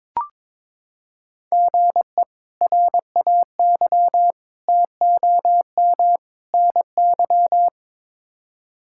Morse code, heard as a keyed tone, about 22 wpm, 700 Hz. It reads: GE RAY TOM NY
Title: K1USN SST QSO Practice - 22wpm